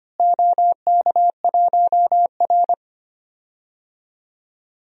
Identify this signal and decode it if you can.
Morse code, heard as a keyed tone, about 25 wpm, 700 Hz. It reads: OK1R
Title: DXCC-100 Call Signs - 4 Characters - Repeated 3x - 25wpm